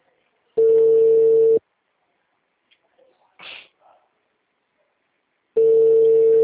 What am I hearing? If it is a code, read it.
Ringback tone (China, 450 Hz).
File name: out- -2010- - - .10